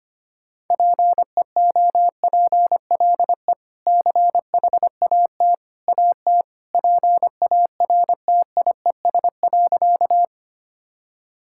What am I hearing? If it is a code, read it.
Morse code, heard as a keyed tone, about 25 wpm, 700 Hz. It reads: PEOPLE CHAT AT PARTIES.
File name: Easy Sentences with Hints - 25wpm